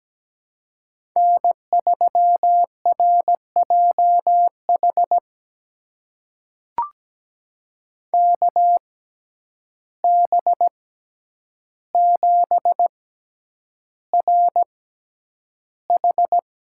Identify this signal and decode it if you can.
Morse code, heard as a keyed tone, about 17 wpm, spaced as for 9 wpm, 700 Hz. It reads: N3RJH KB7RH
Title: USA Call Signs - 5 Characters - 6x Character Spacing - 17wpm